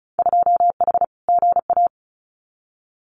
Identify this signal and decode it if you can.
Morse code, heard as a keyed tone, about 36 wpm, 700 Hz. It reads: 2H GA
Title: Winter Field Day Exchanges Encore 35wpm